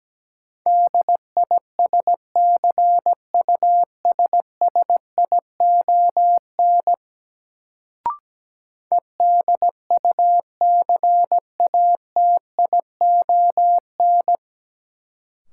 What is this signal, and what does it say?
Morse code, heard as a keyed tone, about 17 wpm, 700 Hz. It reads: DISCUSSION EDUCATION
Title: Sentences from Top 1100 Words - New Word Drill Alt - 17wpm